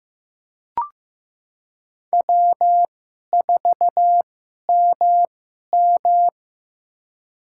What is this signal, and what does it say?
Morse code, heard as a keyed tone, about 15 wpm, spaced as for 11 wpm, 700 Hz. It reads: W4MM
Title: USA Call Signs - 2x Character Spacing - 15wpm